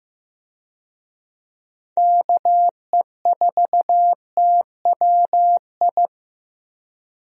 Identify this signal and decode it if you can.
Morse code, heard as a keyed tone, about 15 wpm, 700 Hz. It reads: KE4TWI
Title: USA Call Signs - 6 Characters - Repeated 3x - 15wpm